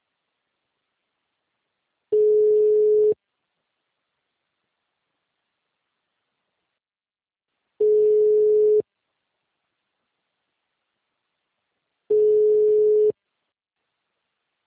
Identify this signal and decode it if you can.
Ringback tone (Europe, 425 Hz).